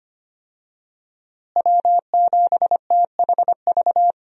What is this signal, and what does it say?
Morse code, heard as a keyed tone, about 25 wpm, 700 Hz. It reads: W7THV
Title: USA Call Signs - 5 Characters - Repeated 2x - 25wpm